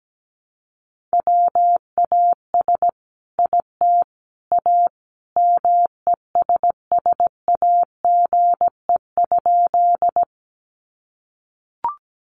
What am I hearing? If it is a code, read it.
Morse code, heard as a keyed tone, about 17 wpm, 700 Hz. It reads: WAS IT A MESSAGE?
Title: Sentences from Top 1100 Words 17wpm